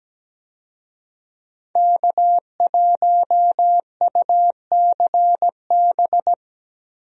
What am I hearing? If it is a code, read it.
Morse code, heard as a keyed tone, about 17 wpm, 700 Hz. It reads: K1UCB
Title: USA Call Signs - 5 Characters - Repeated 2x - 17wpm